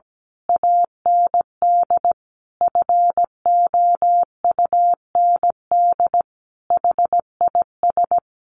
Morse code, heard as a keyed tone, about 17 wpm, 700 Hz. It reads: AND FOUND HIS
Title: A Sound Of Thunder 17wpm